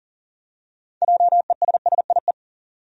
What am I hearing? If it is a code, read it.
Morse code, heard as a keyed tone, about 38 wpm, 700 Hz. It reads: JESSIE
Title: USA Common Names - 40wpm